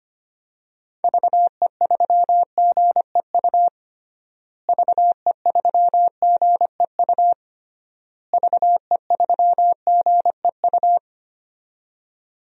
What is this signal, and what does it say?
Morse code, heard as a keyed tone, about 25 wpm, 700 Hz. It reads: VE3GEU VE3GEU VE3GEU
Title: DXCC-100 Call Signs - 6 Characters - Repeated 3x - 25wpm